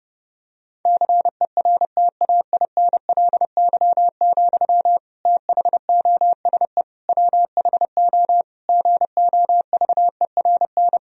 Morse code, heard as a keyed tone, about 32 wpm, 700 Hz. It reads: CERTAINLY, THOSE WHO GOVERN
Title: Sentences from Top 1000 Words - Review Part 2 - 30wpm